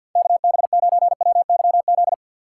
Morse code, heard as a keyed tone, about 50 wpm, 700 Hz. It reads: KB9WYZ
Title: USA Call Signs - 4x Character Spacing - 50wpm